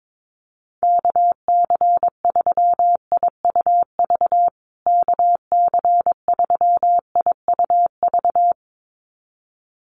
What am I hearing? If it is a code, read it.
Morse code, heard as a keyed tone, about 22 wpm, 700 Hz. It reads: KC3IUV KC3IUV
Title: USA Call Signs Repeated 2x 22wpm